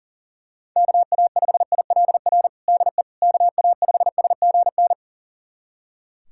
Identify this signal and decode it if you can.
Morse code, heard as a keyed tone, about 40 wpm, 700 Hz. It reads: KA5ILR DE KA5SGN
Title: USA Callsign DE Callsign - 6 Characters - 40wpm